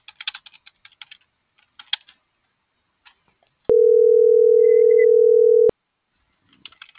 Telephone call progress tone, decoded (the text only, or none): ringback tone